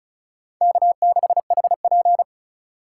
Morse code, heard as a keyed tone, about 43 wpm, 700 Hz. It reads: K6HP
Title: Speed Racing - USA 1x2 Call Signs 35wpm